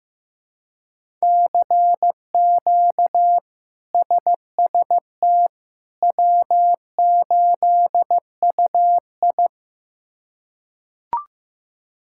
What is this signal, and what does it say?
Morse code, heard as a keyed tone, about 16 wpm, 700 Hz. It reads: CQ SST W8UI
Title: K1USN SST QSO Practice - 15wpm